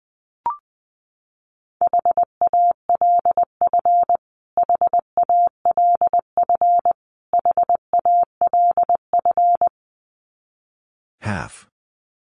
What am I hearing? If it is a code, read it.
Morse code, heard as a keyed tone, about 20 wpm, 700 Hz. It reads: HALF HALF HALF
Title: Top 500 Words Repeated 20wpm